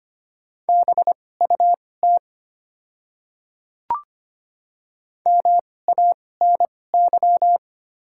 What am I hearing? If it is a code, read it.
Morse code, heard as a keyed tone, about 25 wpm, spaced as for 18 wpm, 690 Hz. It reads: BUT MANY
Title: 2x Character Spacing - Top 100 Words - 25wpm